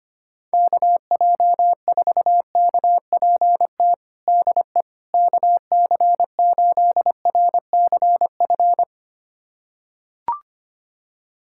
Morse code, heard as a keyed tone, about 25 wpm, 700 Hz. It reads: KJ4KPT DE KC8RCF
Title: USA Callsign DE Callsign - 6 Characters - 25wpm